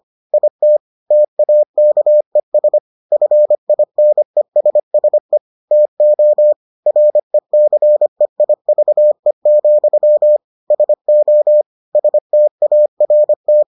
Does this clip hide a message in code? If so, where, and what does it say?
Yes: Morse code from 0.3 to 13.6 s, heard as a keyed tone, about 25 wpm, 600 Hz. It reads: IT TAKES FINESSE TO RECEIVE, SO START